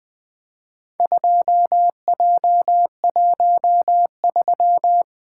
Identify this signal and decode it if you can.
Morse code, heard as a keyed tone, about 21 wpm, 700 Hz. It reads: 2J13
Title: Four Letter-Number Combinations - 20wpm